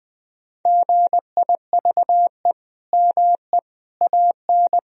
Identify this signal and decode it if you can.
Morse code, heard as a keyed tone, about 21 wpm, 700 Hz. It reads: GIVE ME AN
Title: Sentences from Top 1000 Words - Review Part 3 - 20wpm